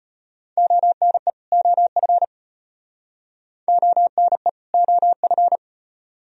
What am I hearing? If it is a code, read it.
Morse code, heard as a keyed tone, about 34 wpm, 700 Hz. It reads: ONE OF ONE OF